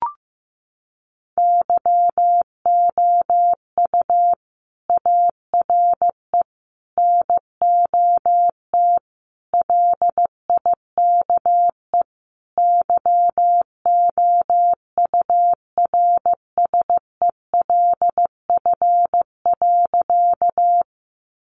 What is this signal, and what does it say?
Morse code, heard as a keyed tone, about 15 wpm, 700 Hz. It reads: YOU ARE NOT LIKE YOURSELF.
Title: Sentences from Top 600 Words 15wpm